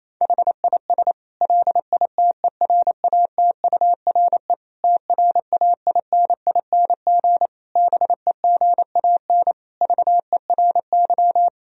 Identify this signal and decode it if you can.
Morse code, heard as a keyed tone, about 28 wpm, 700 Hz. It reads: HIS LITERATURE TRAINING BEGAN VERY
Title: Sentences from Top 1000 Words - Review Part 4 - 28wpm